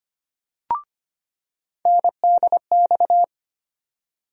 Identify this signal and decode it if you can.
Morse code, heard as a keyed tone, about 25 wpm, 700 Hz. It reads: NDX
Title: USA Call Sign Postfixes - 3 Letters 25wpm